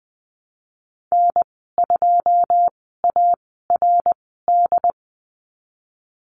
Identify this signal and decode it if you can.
Morse code, heard as a keyed tone, about 20 wpm, spaced as for 15 wpm, 700 Hz. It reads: N2ARD
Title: 082 Lesson 14g - Number 2 - Call Signs 20wpm